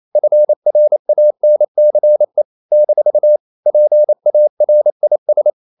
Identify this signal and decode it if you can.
Morse code, heard as a keyed tone, about 28 wpm, 600 Hz. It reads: FRANCE - PARIS
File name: CW / Practice / NationalCapitals28